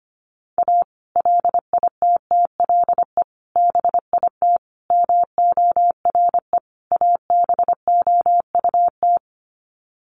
Morse code, heard as a keyed tone, about 25 wpm, 700 Hz. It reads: A LITTLE BIT MORE ABOUT
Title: Sets of 5 Words Encore2 25wpm